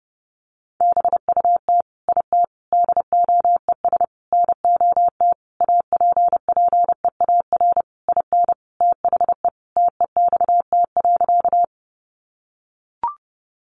Morse code, heard as a keyed tone, about 30 wpm, 700 Hz. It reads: BUT IT DOES NOT APPEAR IN THE TEXT.
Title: Sentences from Top 1200 Words 30wpm